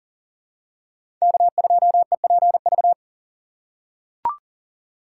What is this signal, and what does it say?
Morse code, heard as a keyed tone, about 40 wpm, 700 Hz. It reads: K2EPV
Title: USA Call Signs - 5 Characters - 4x Character Spacing - 40wpm